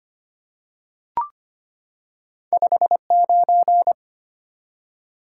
Morse code, heard as a keyed tone, about 25 wpm, 700 Hz. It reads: 59
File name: Two Letter-Number Combinations - 25wpm